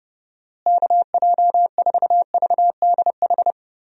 Morse code, heard as a keyed tone, about 32 wpm, 700 Hz. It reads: KJ4VDH
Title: USA Call Signs - 6 Characters - Repeated 2x - 30wpm